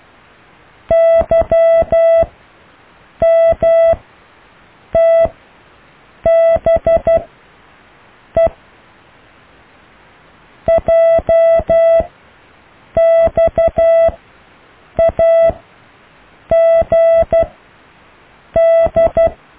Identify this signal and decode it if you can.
Morse code, heard as a keyed tone, about 12 wpm, spaced as for 6 wpm, 660 Hz. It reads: YMTBE JXAGD
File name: cw / Alle letters